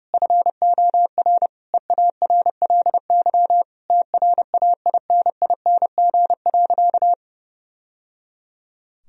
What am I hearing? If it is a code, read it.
Morse code, heard as a keyed tone, about 30 wpm, 700 Hz. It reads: FOR EARLY TRAINING.
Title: Sentences from Top 1000 Words - Review Part 4 - 30wpm